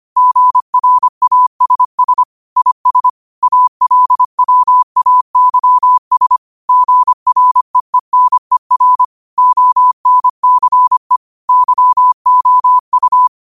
Morse code, heard as a keyed tone, about 25 wpm, 1000 Hz. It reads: GRASS IS ALWAYS GREENER ONCE YOU
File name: qotdfiles / 25wpm